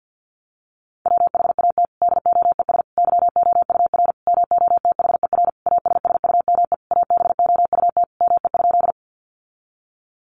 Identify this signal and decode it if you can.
Morse code, heard as a keyed tone, about 50 wpm, 700 Hz. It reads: WHAT DOES YOUR MOTHER ASSUME ABOUT ME?
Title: Sentences from Top 1500 Words 50wpm